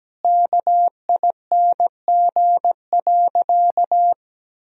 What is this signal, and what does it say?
Morse code, heard as a keyed tone, about 17 wpm, 700 Hz. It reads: KING.
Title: Sentences from Top 1000 Words - Review Part 3 - 17wpm